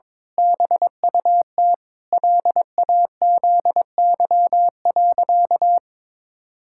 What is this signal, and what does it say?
Morse code, heard as a keyed tone, about 21 wpm, 700 Hz. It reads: BUT LAZY.
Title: Pangram Challenge v2 - 22wpm